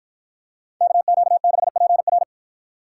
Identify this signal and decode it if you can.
Morse code, heard as a keyed tone, about 47 wpm, 700 Hz. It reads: KQ6PR